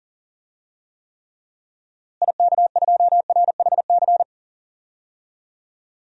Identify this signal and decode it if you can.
Morse code, heard as a keyed tone, about 40 wpm, 700 Hz. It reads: IK2RHC
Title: DXCC-100 Call Signs - 6 Characters - Repeated 2x - 40wpm